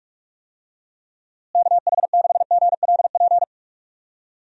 Morse code, heard as a keyed tone, about 42 wpm, 690 Hz. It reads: KH6GLP
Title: DXCC-100 Call Signs - 6 Characters - Repeated 2x - 45wpm